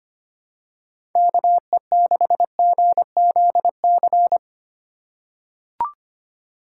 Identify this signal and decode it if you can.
Morse code, heard as a keyed tone, about 25 wpm, 700 Hz. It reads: KE6GZC
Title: USA Call Signs - 6 Characters - 3x Character Spacing - 25wpm